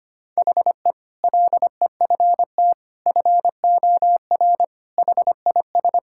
Morse code, heard as a keyed tone, about 25 wpm, 700 Hz. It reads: HE LEFT FOR HIS
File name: practice / 25wpm / Sentences with 2-Word Phrases - 25wpm